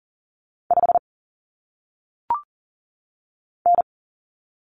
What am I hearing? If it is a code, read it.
Morse code, heard as a keyed tone, about 39 wpm, 700 Hz. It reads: 5 N
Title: Common Cut Numbers - 1-Digit 40wpm